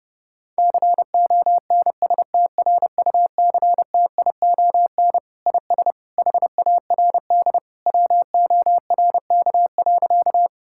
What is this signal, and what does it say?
Morse code, heard as a keyed tone, about 32 wpm, 700 Hz. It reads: CONSTRUCTION IS HARD WORK.
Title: Sentences from Top 1600 Words 30wpm